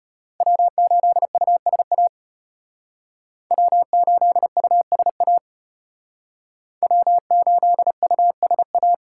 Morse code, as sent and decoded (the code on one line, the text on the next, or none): .-- ---.. ..- ... .- / .-- ---.. ..- ... .- / .-- ---.. ..- ... .-
W8USA W8USA W8USA